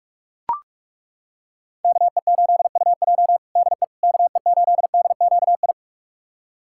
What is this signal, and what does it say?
Morse code, heard as a keyed tone, about 42 wpm, 700 Hz. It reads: KE8UJ DE KE8DQI
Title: USA Callsign DE Callsign 45wpm